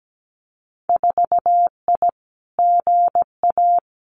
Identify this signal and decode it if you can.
Morse code, heard as a keyed tone, about 17 wpm, 700 Hz. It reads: 4I GA
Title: Winter Field Day Exchanges Encore 17wpm